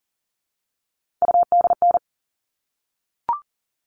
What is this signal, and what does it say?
Morse code, heard as a keyed tone, about 40 wpm, 700 Hz. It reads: UDN
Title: All Cut Numbers - 3-Digit 40wpm